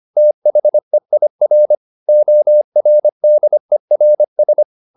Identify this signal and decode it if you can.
Morse code, heard as a keyed tone, about 25 wpm, 600 Hz. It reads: THEIR ORDERS
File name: the old man and the sea part 1 25wpm